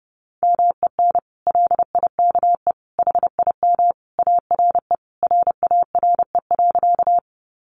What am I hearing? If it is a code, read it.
Morse code, heard as a keyed tone, about 28 wpm, 700 Hz. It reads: MEN LIKE HIM ARE RARE.
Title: Sentences from Top 1100 Words 30wpm